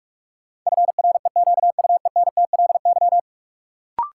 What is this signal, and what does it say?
Morse code, heard as a keyed tone, about 47 wpm, 700 Hz. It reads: FREQUENTLY
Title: Sentences from Top 1300 Words - New Word Drill Alt - 45wpm